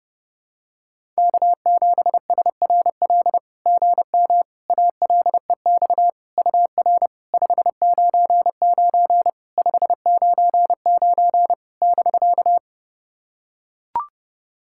Morse code, heard as a keyed tone, about 30 wpm, 660 Hz. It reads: K7SRL GM ALEX UR 599 599 <BK>
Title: SOTA QSO Practice 30wpm